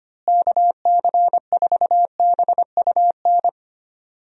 Morse code, heard as a keyed tone, about 25 wpm, 700 Hz. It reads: KC4BUN
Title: USA Call Signs - 6 Characters - Repeated 2x - 25wpm